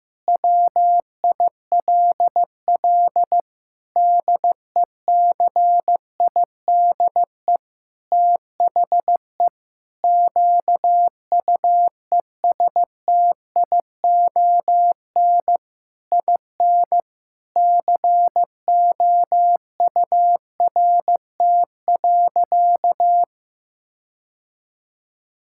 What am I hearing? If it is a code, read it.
Morse code, heard as a keyed tone, about 15 wpm, 700 Hz. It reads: WILL DECIDE THE QUESTION IN COURT.
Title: Sentences from Top 1600 Words 15wpm